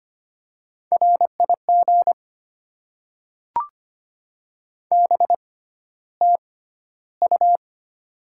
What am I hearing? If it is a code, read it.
Morse code, heard as a keyed tone, about 25 wpm, spaced as for 13 wpm, 690 Hz. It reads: RIG BTU
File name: Top 20 QSO Elements - 6x Character Spacing - 25wpm